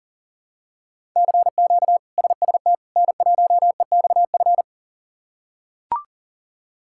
Morse code, heard as a keyed tone, about 37 wpm, 700 Hz. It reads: CQ SST N1EXF
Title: K1USN SST QSO Practice - 40wpm